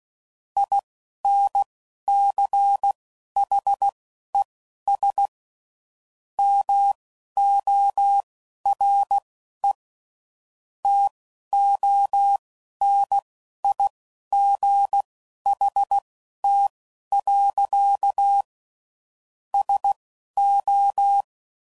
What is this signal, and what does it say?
Morse code, heard as a keyed tone, about 16 wpm, spaced as for 11 wpm, 800 Hz. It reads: INCHES MORE TONIGHT. SO
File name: CW Code Groups @10WPM / QSOs / QSO WA7QOX